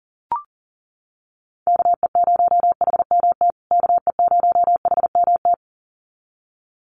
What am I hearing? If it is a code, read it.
Morse code, heard as a keyed tone, about 40 wpm, 700 Hz. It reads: KE0HMT KE0HMT
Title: USA Call Signs Repeated 2x 40wpm